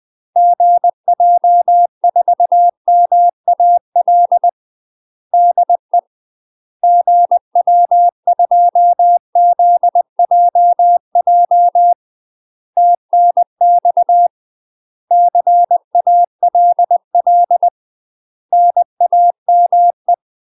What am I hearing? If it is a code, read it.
Morse code, heard as a keyed tone, about 20 wpm, 700 Hz. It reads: GJ4MAL DE GW2ZJJ TNX CALL NAME
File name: UK MASTER QSOs TRAINING / 20WPM / UK MASTER QSOs 0031